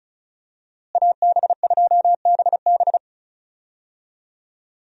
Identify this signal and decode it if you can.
Morse code, heard as a keyed tone, about 36 wpm, 700 Hz. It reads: AB2BB
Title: USA Call Signs - 5 Characters - Repeated 3x - 35wpm